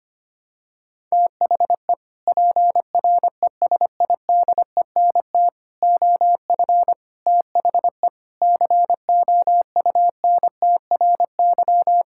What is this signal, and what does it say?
Morse code, heard as a keyed tone, about 25 wpm, 700 Hz. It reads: THE PRESIDENT OF THE COUNTRY
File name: practice / 25wpm / Sentences with 3-Word Phrases - 25wpm